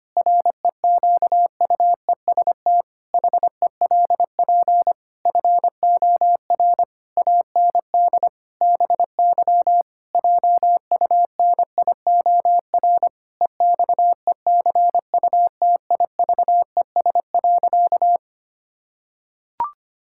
Morse code, heard as a keyed tone, about 25 wpm, 700 Hz. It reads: REQUEST HELP FOR AND BY JUNIOR EXECUTIVES.
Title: Pangram Challenge v2 - 25wpm